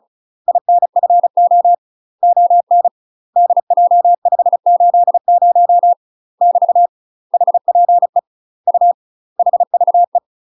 Morse code, heard as a keyed tone, about 36 wpm, 700 Hz. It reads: INFO ON DJ580 <BT> HPE U HVE
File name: UK MASTER QSOs TRAINING / 35WPM / UK MASTER QSOs 0029